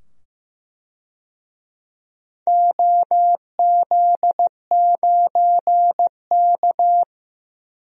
Morse code, heard as a keyed tone, about 15 wpm, 700 Hz. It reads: OZ9K
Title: DXCC-100 Call Signs - 4 Characters - 15wpm